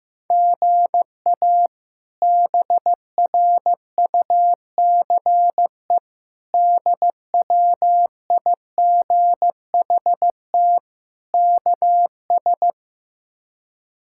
Morse code, heard as a keyed tone, about 15 wpm, 700 Hz. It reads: GA BRUCE DWIGHT KS
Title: K1USN SST QSO Practice - 15wpm